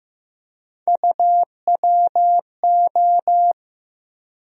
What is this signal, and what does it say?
Morse code, heard as a keyed tone, about 15 wpm, 700 Hz. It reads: UWO
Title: USA Call Sign Postfixes - 3 Letters 15wpm